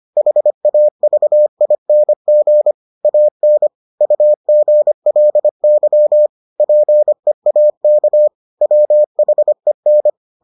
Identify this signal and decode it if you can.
Morse code, heard as a keyed tone, about 25 wpm, 600 Hz. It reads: HAVING AN UGLY PEAK WHEN